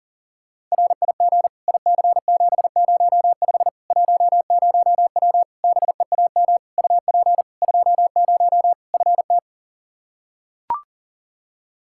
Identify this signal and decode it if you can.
Morse code, heard as a keyed tone, about 40 wpm, 700 Hz. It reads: RIG IC705 10W BEAM UP 20 FT
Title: Practice QSO - Exchange 2 - Info - 40wpm